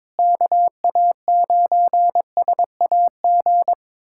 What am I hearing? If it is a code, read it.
Morse code, heard as a keyed tone, about 22 wpm, 700 Hz. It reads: KA9SAG
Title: USA Callsign DE Callsign - 6 Characters - 22wpm